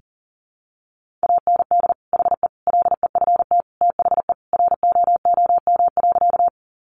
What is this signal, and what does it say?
Morse code, heard as a keyed tone, about 40 wpm, 700 Hz. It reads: AND HE LEFT THE ROOM.
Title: Sentences from Top 300 Words 40wpm